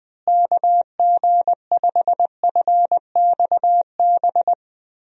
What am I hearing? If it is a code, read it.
Morse code, heard as a keyed tone, about 21 wpm, 700 Hz. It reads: KG5FXB
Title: USA Call Signs - 6 Characters - Repeated 3x - 20wpm